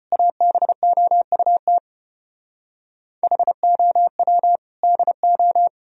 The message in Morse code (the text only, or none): ABOUT HOW DO